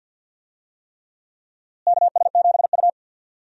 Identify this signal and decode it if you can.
Morse code, heard as a keyed tone, about 52 wpm, 700 Hz. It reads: KS7U
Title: USA Call Signs - 4 Characters - Repeated 3x - 50wpm